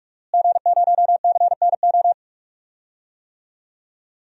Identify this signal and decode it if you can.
Morse code, heard as a keyed tone, about 41 wpm, 700 Hz. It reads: G0CNO
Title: DXCC-100 Call Signs - 5 Characters - Repeated 3x - 45wpm